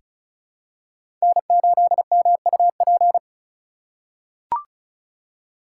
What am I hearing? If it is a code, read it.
Morse code, heard as a keyed tone, about 36 wpm, 700 Hz. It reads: N8MUP
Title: USA Call Signs - 5 Characters - 6x Character Spacing - 35wpm